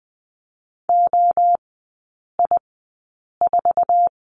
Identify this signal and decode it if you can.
Morse code, heard as a keyed tone, about 19 wpm, spaced as for 8 wpm, 700 Hz. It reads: OI4
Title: 067 Lesson 12e - Letter d - Three Characters 20wpm